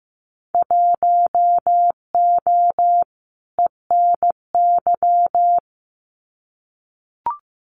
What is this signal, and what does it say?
Morse code, heard as a keyed tone, about 15 wpm, 700 Hz. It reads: 1O ENY
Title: Winter Field Day Exchanges 15wpm